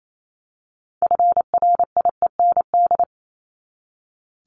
Morse code, heard as a keyed tone, about 28 wpm, 700 Hz. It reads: FRIEND
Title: Top 500 Words 28wpm